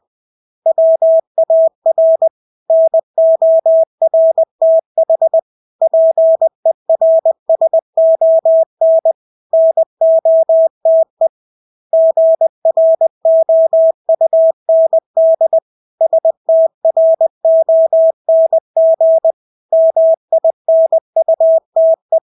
Morse code, heard as a keyed tone, about 20 wpm, 650 Hz. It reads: WAR NORTH PERSON NOTE GROUND STRONG MINUTE
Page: Engelske ord 20 wpm